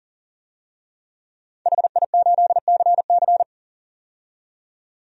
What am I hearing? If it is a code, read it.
Morse code, heard as a keyed tone, about 38 wpm, 700 Hz. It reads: HI8CC
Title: DXCC-100 Call Signs - 5 Characters - 40wpm